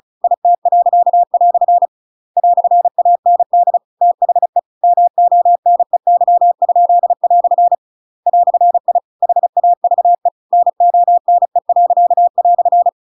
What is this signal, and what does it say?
Morse code, heard as a keyed tone, about 34 wpm, 700 Hz. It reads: IT." "AND THE MONEY?" "I HAVE NONE."
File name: Pinocchio / Pinocchio ch8 35 35 ik1hge